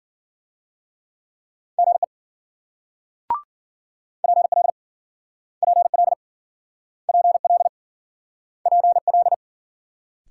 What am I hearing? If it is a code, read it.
Morse code, heard as a keyed tone, about 53 wpm, 700 Hz. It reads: GE PL PL PL PL